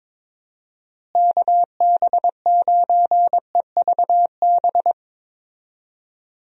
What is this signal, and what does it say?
Morse code, heard as a keyed tone, about 22 wpm, 700 Hz. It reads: KB9EVB
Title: USA Call Signs - 6 Characters - Repeated 3x - 22wpm